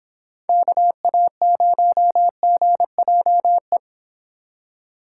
Morse code, heard as a keyed tone, about 25 wpm, 700 Hz. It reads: KA0GJE